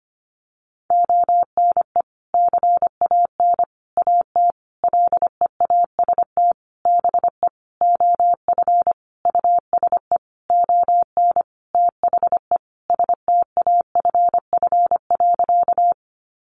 Morse code, heard as a keyed tone, about 25 wpm, 700 Hz. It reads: ONE CAN AT LEAST BE OF USE ON THE STAFF.
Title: Sentences from Top 1200 Words 25wpm